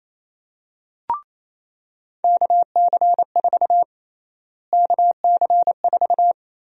Morse code, heard as a keyed tone, about 28 wpm, 700 Hz. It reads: KC4 KC4
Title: USA Call Sign Prefixes - Repeated 2x - 28wpm